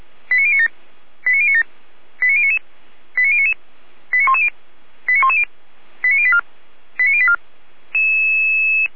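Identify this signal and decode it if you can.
Five-tone selective call (ZVEI), heard as quick runs of five tones, tones of about 80 ms. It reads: 89097 89097 89099 89099 89110 89110 89084 89084
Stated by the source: ZVEI five-tone selective call, code 89080, received with an SDR.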